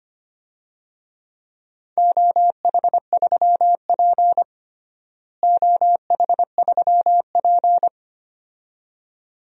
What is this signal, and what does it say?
Morse code, heard as a keyed tone, about 25 wpm, 700 Hz. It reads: OH3P OH3P
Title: DXCC-100 Call Signs - 4 Characters - Repeated 2x - 25wpm